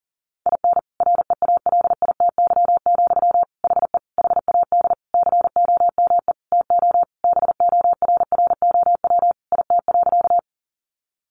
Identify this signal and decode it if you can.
Morse code, heard as a keyed tone, about 40 wpm, 700 Hz. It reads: IN REALITY, HE HAD COME TO BORROW IT.
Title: Sentences from Top 1500 Words 40wpm